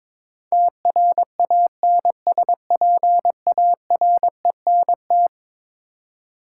Morse code, heard as a keyed tone, about 22 wpm, 700 Hz. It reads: TRANSPARENT
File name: practice / 22wpm / Eleven Letter Words - 22wpm